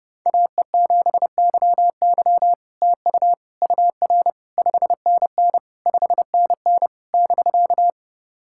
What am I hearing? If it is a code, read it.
Morse code, heard as a keyed tone, about 29 wpm, 700 Hz. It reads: AE7YY TU UR 5NN 5NN <BK>
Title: SOTA QSO Practice 30wpm